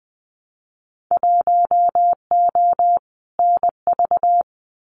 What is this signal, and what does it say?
Morse code, heard as a keyed tone, about 20 wpm, 700 Hz. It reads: 1O NV
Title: Winter Field Day Exchanges Encore 20wpm